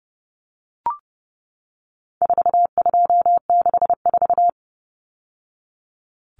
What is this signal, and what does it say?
Morse code, heard as a keyed tone, about 30 wpm, 700 Hz. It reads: 4264
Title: Numbers - 4-Digit 30wpm